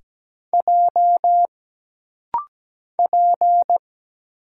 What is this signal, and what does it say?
Morse code, heard as a keyed tone, about 17 wpm, 700 Hz. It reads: J P
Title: Single Letter-Number Alt - Rapid-Fire 17wpm